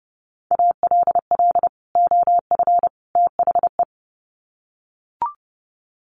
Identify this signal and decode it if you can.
Morse code, heard as a keyed tone, about 30 wpm, 700 Hz. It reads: ALL OF THE
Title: Sets of 3 Words 30wpm